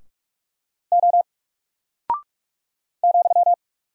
Morse code, heard as a keyed tone, about 44 wpm, 700 Hz. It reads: O ,
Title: Single Character (Letter:Number:Punctuation:Prosign) 45wpm